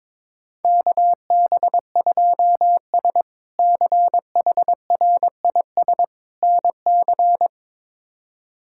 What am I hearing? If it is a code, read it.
Morse code, heard as a keyed tone, about 22 wpm, 700 Hz. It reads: KB2S CHRIS NC
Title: K1USN SST QSO Practice - 22wpm